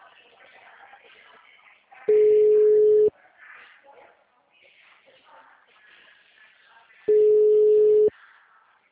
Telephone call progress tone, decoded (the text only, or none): ringback tone